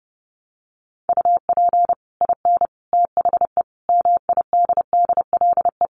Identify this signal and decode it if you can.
Morse code, heard as a keyed tone, about 30 wpm, 700 Hz. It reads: UP IN THE MIDDLE
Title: Sets of 5 Words Encore 30wpm